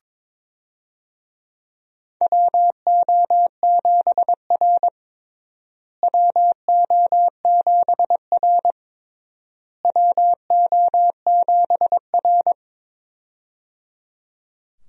Morse code, heard as a keyed tone, about 22 wpm, 700 Hz. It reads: WO7R WO7R WO7R
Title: USA Call Signs - 4 Characters - Repeated 3x - 22wpm